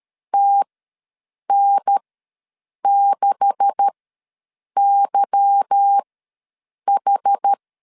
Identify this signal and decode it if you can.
Morse code, heard as a keyed tone, about 13 wpm, spaced as for 7 wpm, 800 Hz. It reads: TN6YH
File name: LEZIONI / 9 286 ws 12wpm 3351